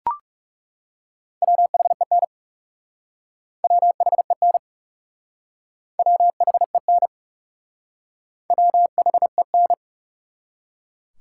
Morse code, heard as a keyed tone, about 38 wpm, 700 Hz. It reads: WHEN WHEN WHEN WHEN